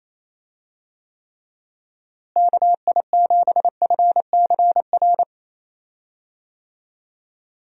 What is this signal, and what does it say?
Morse code, heard as a keyed tone, about 28 wpm, 700 Hz. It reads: KI7FCR
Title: USA Call Signs - 6 Characters - Repeated 2x - 28wpm